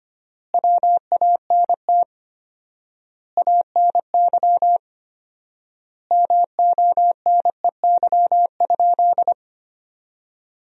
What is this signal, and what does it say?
Morse code, heard as a keyed tone, about 25 wpm, 700 Hz. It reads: WANT ANY MONEY?
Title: 4x Word-Spacing - Sentences from Top 400 Words - 25wpm